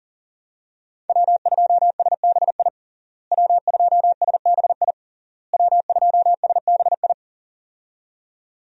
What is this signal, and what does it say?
Morse code, heard as a keyed tone, about 40 wpm, 700 Hz. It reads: W2SBI W2SBI W2SBI
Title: USA Call Signs - 5 Characters - Repeated 3x - 40wpm